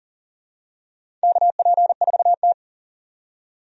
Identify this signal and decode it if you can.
Morse code, heard as a keyed tone, about 40 wpm, 700 Hz. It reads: KP4T
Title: DXCC-100 Call Signs - 4 Characters - 40wpm